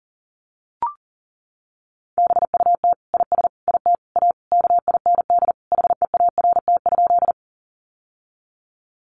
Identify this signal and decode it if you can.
Morse code, heard as a keyed tone, about 40 wpm, 700 Hz. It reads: BUT IS IT A KIND HEART?
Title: Sentences from Top 1200 Words 40wpm